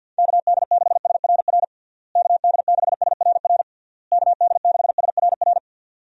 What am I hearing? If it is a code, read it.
Morse code, heard as a keyed tone, about 50 wpm, 700 Hz. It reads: KD6SRR KD6SRR KD6SRR
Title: USA Call Signs - 6 Characters - Repeated 3x - 50wpm